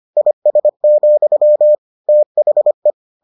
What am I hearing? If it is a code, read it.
Morse code, heard as a keyed tone, about 25 wpm, 600 Hz. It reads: IS, THE